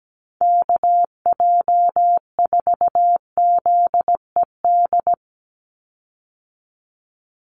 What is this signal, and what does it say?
Morse code, heard as a keyed tone, about 17 wpm, 700 Hz. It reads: KJ4ZED
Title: USA Call Signs 17wpm